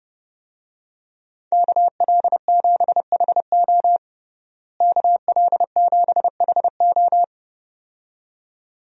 Morse code, heard as a keyed tone, about 31 wpm, 700 Hz. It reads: KL7HO KL7HO
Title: DXCC-100 Call Signs - 5 Characters - Repeated 2x - 30wpm